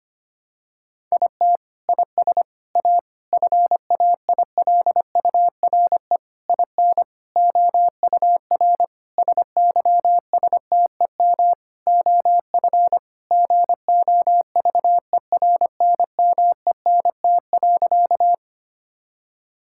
Morse code, heard as a keyed tone, about 25 wpm, 700 Hz. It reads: IT IS A FAILURE IN OUR SYSTEM OF GOVERNMENT.
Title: Sentences from Top 1000 Words - Review Part 6 - 25wpm